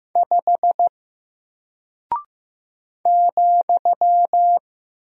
Morse code, heard as a keyed tone, about 15 wpm, 700 Hz. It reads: 5 ,
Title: Single Character (Letter:Number:Punctuation:Prosign) Alt 15wpm